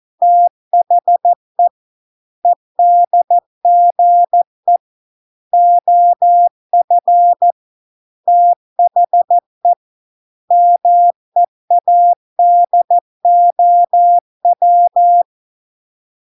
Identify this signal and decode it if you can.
Morse code, heard as a keyed tone, about 14 wpm, 700 Hz. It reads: THE EDGE OF THE MEADOW